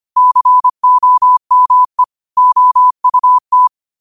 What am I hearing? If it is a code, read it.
Morse code, heard as a keyed tone, about 25 wpm, 1000 Hz. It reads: COME OUT
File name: qotdfiles / 25wpm